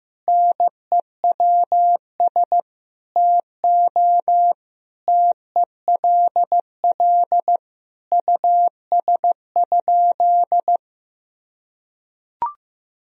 Morse code, heard as a keyed tone, about 15 wpm, 700 Hz. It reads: NEWS TO TELL US?
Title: Sentences from Top 1000 Words - Review Part 4 - 15wpm